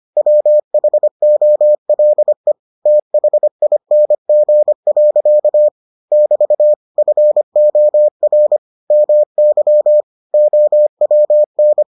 Morse code, heard as a keyed tone, about 25 wpm, 600 Hz. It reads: WHOLE THING. <BT> FOR MY OWN